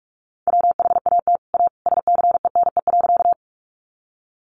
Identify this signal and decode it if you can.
Morse code, heard as a keyed tone, about 44 wpm, 700 Hz. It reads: WHAT A SCENE.
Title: Sentences from Top 1100 Words 45wpm